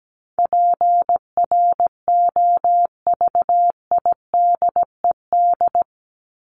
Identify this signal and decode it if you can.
Morse code, heard as a keyed tone, about 17 wpm, 700 Hz. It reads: PROVIDED
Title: Top 1751-2000 Words 17wpm